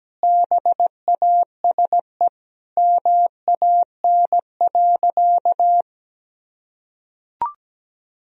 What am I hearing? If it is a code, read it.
Morse code, heard as a keyed tone, about 17 wpm, 700 Hz. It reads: BASE MAN.
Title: Sentences from Top 1000 Words - Review Part 1 - 17wpm